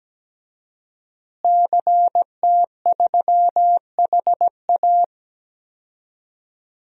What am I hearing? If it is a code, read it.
Morse code, heard as a keyed tone, about 17 wpm, 700 Hz. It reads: CT3HA
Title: DXCC-100 Call Signs - 5 Characters - Repeated 2x - 17wpm